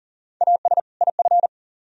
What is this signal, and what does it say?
Morse code, heard as a keyed tone, about 51 wpm, 700 Hz. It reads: AS IF